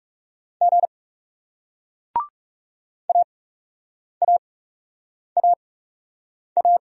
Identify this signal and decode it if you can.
Morse code, heard as a keyed tone, about 39 wpm, spaced as for 8 wpm, 700 Hz. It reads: G AAAA